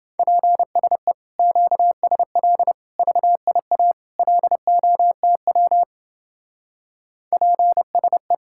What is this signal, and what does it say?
Morse code, heard as a keyed tone, about 28 wpm, 700 Hz. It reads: PSE QSL VIA LOTW PSE